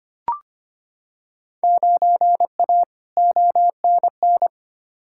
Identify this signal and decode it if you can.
Morse code, heard as a keyed tone, about 25 wpm, 700 Hz. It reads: 9A ONN
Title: Field Day Exchanges Encore - 25wpm